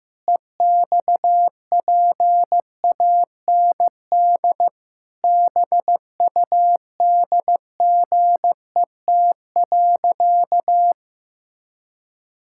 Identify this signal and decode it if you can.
Morse code, heard as a keyed tone, about 15 wpm, 700 Hz. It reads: EXPAND BUDGET.
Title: Pangram Challenge v2 - 15wpm